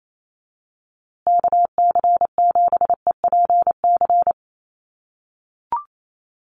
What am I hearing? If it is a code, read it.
Morse code, heard as a keyed tone, about 28 wpm, 700 Hz. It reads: KC7EPC
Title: USA Call Signs Repeated 3x 28wpm